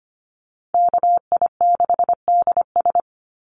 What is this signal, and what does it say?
Morse code, heard as a keyed tone, about 25 wpm, 700 Hz. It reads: KI6DS
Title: NA SOTA Chaser Callsigns 25wpm